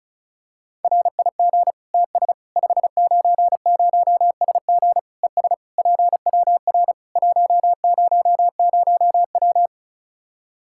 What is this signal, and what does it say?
Morse code, heard as a keyed tone, about 34 wpm, 700 Hz. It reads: RIG TS 590SG ES PWR 100W
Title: Practice QSO - Exchange 2 - Info - 35wpm